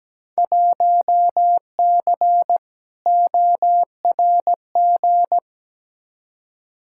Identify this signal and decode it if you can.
Morse code, heard as a keyed tone, about 17 wpm, 700 Hz. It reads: 1C ORG
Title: Field Day Exchanges - 17wpm